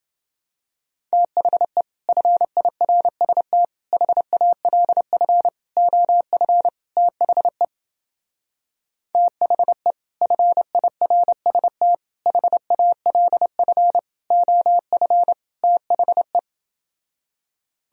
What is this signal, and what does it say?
Morse code, heard as a keyed tone, about 28 wpm, 700 Hz. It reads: THE FIRST HALF OF THE THE FIRST HALF OF THE